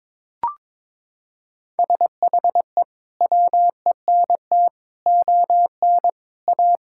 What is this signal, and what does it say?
Morse code, heard as a keyed tone, about 22 wpm, 700 Hz. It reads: SHE WENT ON A
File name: practice / 22wpm / Sentences with 2-Word Phrases - 22wpm